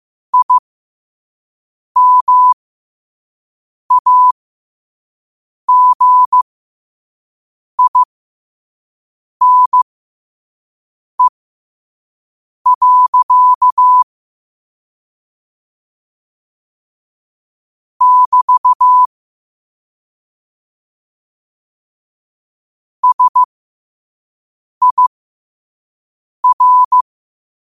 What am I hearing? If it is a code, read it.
Morse code, heard as a keyed tone, about 15 wpm, spaced as for 5 wpm, 1000 Hz. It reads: IMAGINE. <BT> SIR
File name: 5wpm